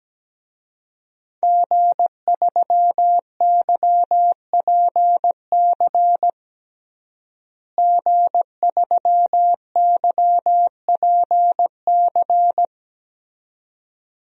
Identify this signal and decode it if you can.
Morse code, heard as a keyed tone, about 17 wpm, 700 Hz. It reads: G3YPC G3YPC
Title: DXCC-100 Call Signs - 5 Characters - Repeated 2x - 17wpm